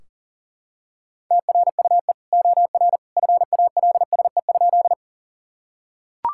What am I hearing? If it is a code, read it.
Morse code, heard as a keyed tone, about 40 wpm, 700 Hz. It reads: TRUE OR FALSE?
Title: Sentences from Top 1600 Words 40wpm